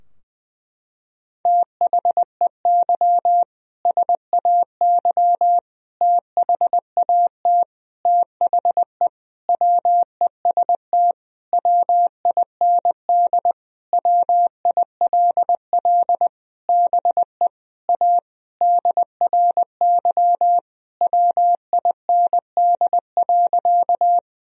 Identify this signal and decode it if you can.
Morse code, heard as a keyed tone, about 21 wpm, 700 Hz. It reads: THEY SAY THAT THE WEST WIND WILL BE A DRY WIND.
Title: Sentences from Top 1000 Words - Review Part 3 - 20wpm